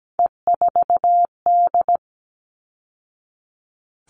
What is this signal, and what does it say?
Morse code, heard as a keyed tone, about 17 wpm, 700 Hz. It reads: E4D
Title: All Cut Numbers - 3-Digit 17wpm